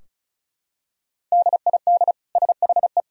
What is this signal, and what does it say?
Morse code, heard as a keyed tone, about 34 wpm, 700 Hz. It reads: DID SHE
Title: Sentences from Top 1000 Words - Review Part 2 - 35wpm